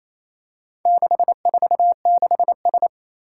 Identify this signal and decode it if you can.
Morse code, heard as a keyed tone, about 28 wpm, 700 Hz. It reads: 646S
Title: Four Challenging Letter-Number Combinations 28wpm